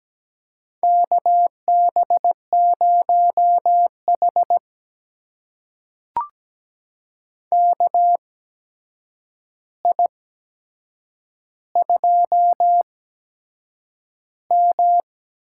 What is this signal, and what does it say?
Morse code, heard as a keyed tone, about 17 wpm, spaced as for 7 wpm, 700 Hz. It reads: KB0H KI2M
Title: USA Call Signs - 4 Characters - 8x Character Spacing - 17wpm